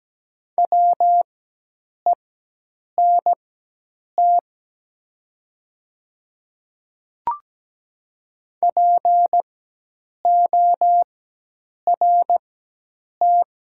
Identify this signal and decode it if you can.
Morse code, heard as a keyed tone, about 17 wpm, spaced as for 8 wpm, 700 Hz. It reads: WENT PORT
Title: 4x Character Spacing - Top 200 Words - 17wpm